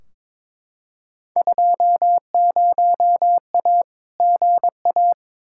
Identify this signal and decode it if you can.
Morse code, heard as a keyed tone, about 22 wpm, 700 Hz. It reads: 20A GA
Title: Field Day Exchanges Encore - 22wpm